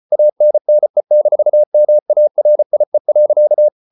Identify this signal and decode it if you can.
Morse code, heard as a keyed tone, about 34 wpm, 600 Hz. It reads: ANNE-MARIE.